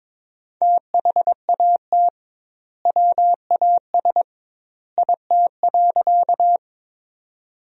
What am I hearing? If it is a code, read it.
Morse code, heard as a keyed tone, about 22 wpm, 700 Hz. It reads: THAT WAS IT.
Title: 2x Word-Spacing - Sentences from Top 100 Words - 22wpm